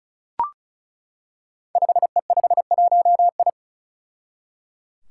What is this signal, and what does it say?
Morse code, heard as a keyed tone, about 36 wpm, 700 Hz. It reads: 5E51I
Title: Five Challenging Letter-Number Combinations 35wpm